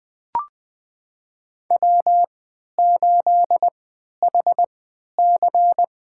Morse code, heard as a keyed tone, about 20 wpm, spaced as for 11 wpm, 700 Hz. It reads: W8HC
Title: USA Call Signs - 4 Characters - 3x Character Spacing - 20wpm